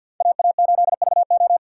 Morse code, heard as a keyed tone, about 49 wpm, 700 Hz. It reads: AA8VO
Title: USA Call Signs - 5 Characters - 4x Character Spacing - 50wpm